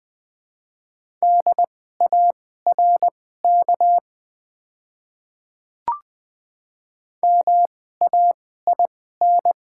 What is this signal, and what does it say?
Morse code, heard as a keyed tone, about 20 wpm, spaced as for 14 wpm, 700 Hz. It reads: DARK MAIN
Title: 2x Character Spacing - Top 500 Words - 20wpm